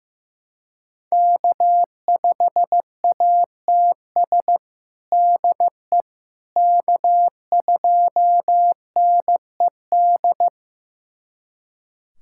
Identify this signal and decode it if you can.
Morse code, heard as a keyed tone, about 15 wpm, 700 Hz. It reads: K5ATS DE K2NED
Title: USA Callsign DE Callsign - 5 Characters - 15wpm